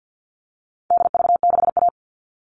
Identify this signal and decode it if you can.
Morse code, heard as a keyed tone, about 50 wpm, 700 Hz. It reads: DV6A
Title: All Cut Numbers - 4-Digit 50wpm